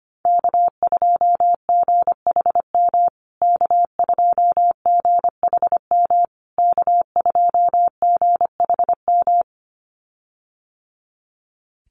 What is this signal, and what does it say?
Morse code, heard as a keyed tone, about 25 wpm, 700 Hz. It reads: K2GHM K2GHM K2GHM
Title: USA Call Signs Repeated 3x 25wpm